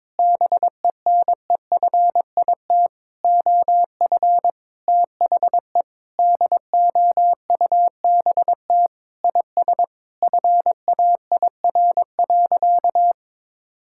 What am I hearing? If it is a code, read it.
Morse code, heard as a keyed tone, about 22 wpm, 700 Hz. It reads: BENEFIT OF THE DOUBT IS FAIR.
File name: practice / 22wpm / Sentences with 8-Word Phrases - 22wpm